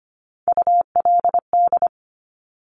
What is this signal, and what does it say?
Morse code, heard as a keyed tone, about 25 wpm, 700 Hz. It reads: ULD
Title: Top Three Letter Combinations in Words 25wpm